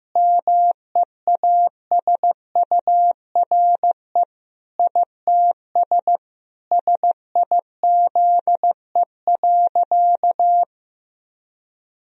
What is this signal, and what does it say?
Morse code, heard as a keyed tone, about 15 wpm, 700 Hz. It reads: MEASURE ITS SIZE.
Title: Sentences from Top 1000 Words - Review Part 2 - 15wpm